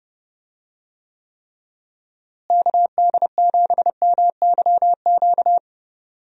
Morse code, heard as a keyed tone, about 30 wpm, 700 Hz. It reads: KD7MYQ
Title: USA Call Signs - 6 Characters - Repeated 3x - 30wpm